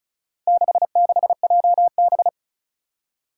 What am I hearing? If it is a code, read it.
Morse code, heard as a keyed tone, about 36 wpm, 700 Hz. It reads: 66JB
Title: Four Challenging Letter-Number Combinations 35wpm